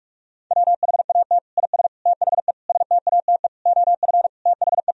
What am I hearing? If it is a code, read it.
Morse code, heard as a keyed tone, about 46 wpm, 700 Hz. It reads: WHAT IS THE STATE OF THE
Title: Sentences from Top 1000 Words - Review Part 3 - 45wpm